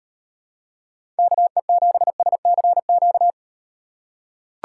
Morse code, heard as a keyed tone, about 38 wpm, 700 Hz. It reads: KE7SCQ